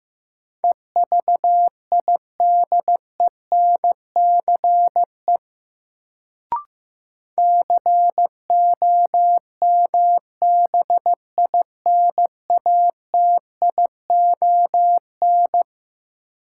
Morse code, heard as a keyed tone, about 15 wpm, 700 Hz. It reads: EVIDENCE COMBINATION
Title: Sentences from Top 1300 Words - New Word Drill Alt - 15wpm